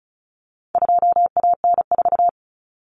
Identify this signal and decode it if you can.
Morse code, heard as a keyed tone, about 34 wpm, 700 Hz. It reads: 2AN4
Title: Common Cut Numbers - 4-Digit 35wpm